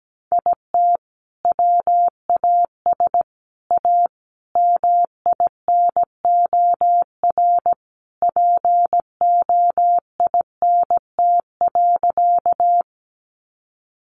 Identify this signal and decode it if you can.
Morse code, heard as a keyed tone, about 17 wpm, 700 Hz. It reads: IT WAS A MINOR POINT.
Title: Sentences from Top 1500 Words 17wpm